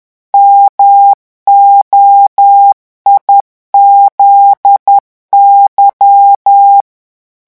Morse code, heard as a keyed tone, about 10 wpm, 800 Hz. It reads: MOIZY
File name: morse mp3 / revision11 lettres vitesse 10